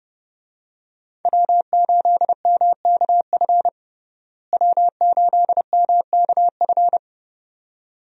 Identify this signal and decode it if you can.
Morse code, heard as a keyed tone, about 30 wpm, 700 Hz. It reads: W8MKF W8MKF
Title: USA Call Signs - 5 Characters - Repeated 2x - 30wpm